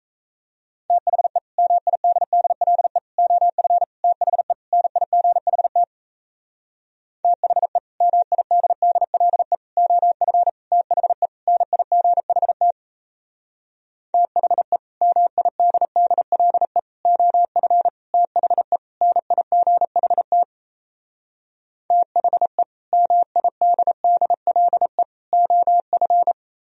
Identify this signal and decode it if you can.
Morse code, heard as a keyed tone, about 37 wpm, 700 Hz. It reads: THE MIDDLE OF THE NIGHT THE MIDDLE OF THE NIGHT THE MIDDLE OF THE NIGHT THE MIDDLE OF